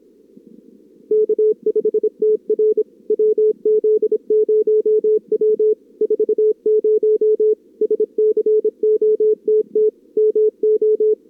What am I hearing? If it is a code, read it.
Morse code, heard as a keyed tone, about 26 wpm, 440 Hz. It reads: K5TR WZ0W 40 SCOTT MO
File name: CWSprint / Feb2022 / k5tr-feb-na-sprint-cw-k5ot -0000